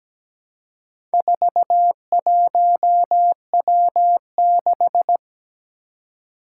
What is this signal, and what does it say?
Morse code, heard as a keyed tone, about 17 wpm, 700 Hz. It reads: 41W6
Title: Four Challenging Letter-Number Combinations 17wpm